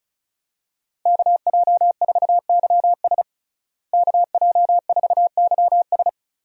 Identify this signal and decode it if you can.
Morse code, heard as a keyed tone, about 36 wpm, 700 Hz. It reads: KJ4YS KJ4YS
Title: USA Call Signs - 5 Characters - Repeated 3x - 35wpm